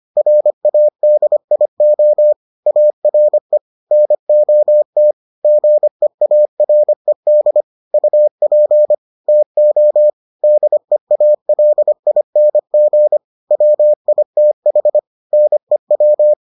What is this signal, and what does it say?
Morse code, heard as a keyed tone, about 25 wpm, 600 Hz. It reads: RADIO ARE NOT GEARED UP TO DEALING WITH NEW